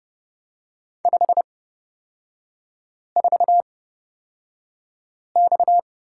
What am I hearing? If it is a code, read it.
Morse code, heard as a keyed tone, about 29 wpm, spaced as for 5 wpm, 700 Hz. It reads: 54X
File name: Lesson 37e - Letter x - Three Characters 30wpm